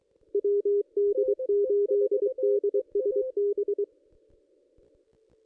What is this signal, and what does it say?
Morse code, heard as a keyed tone, about 24 wpm, 440 Hz. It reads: WD8DSB
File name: contests / ss-cw-k5tr-op-n5rz -1600